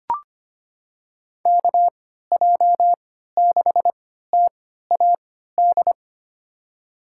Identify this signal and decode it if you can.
Morse code, heard as a keyed tone, about 25 wpm, spaced as for 14 wpm, 700 Hz. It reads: KJ6TAD
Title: USA Call Signs - 3x Character Spacing - 25wpm